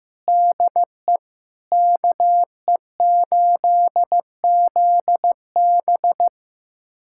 Morse code, heard as a keyed tone, about 15 wpm, 700 Hz. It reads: DE KE8ZB
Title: USA Callsign DE Callsign - 5 Characters - 15wpm